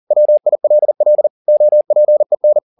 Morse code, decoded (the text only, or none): WILL OPEN